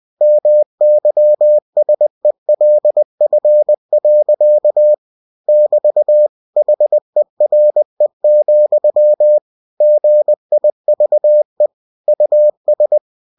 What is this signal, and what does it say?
Morse code, heard as a keyed tone, about 20 wpm, 600 Hz. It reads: MYSELF. <BT> HERE, GIVE US
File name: cw-books / winnie-the-pooh / winnie-20wpm-page-0017